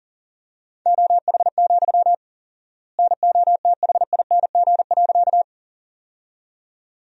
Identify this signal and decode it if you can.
Morse code, heard as a keyed tone, about 40 wpm, 700 Hz. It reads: OH, NOTHING.
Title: 4x Word-Spacing - Sentences from Top 500 Words - 40wpm